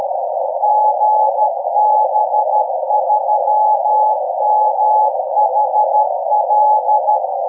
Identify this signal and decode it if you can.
Morse code, heard as a keyed tone, about 13 wpm, 850 Hz, with noise. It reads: GB3MHL
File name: beacon